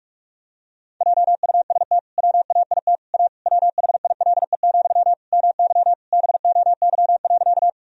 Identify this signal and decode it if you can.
Morse code, heard as a keyed tone, about 46 wpm, 700 Hz. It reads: JUST WAIT A WHILE, MY BOY.
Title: Sentences from Top 1000 Words - Review Part 1 - 45wpm